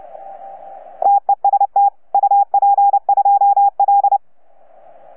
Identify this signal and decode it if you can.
Morse code, heard as a keyed tone, about 29 wpm, 810 Hz. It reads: TEST UP2L
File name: UP2L-20-CW